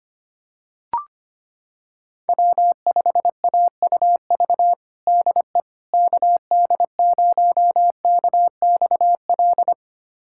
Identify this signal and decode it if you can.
Morse code, heard as a keyed tone, about 25 wpm, 700 Hz. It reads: W5AUV DE KD0KXL
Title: USA Callsign DE Callsign 25wpm